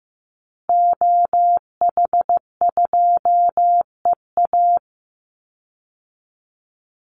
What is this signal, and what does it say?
Morse code, heard as a keyed tone, about 15 wpm, 700 Hz. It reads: OH2EA
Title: Call Signs Repeated 2x - 15wpm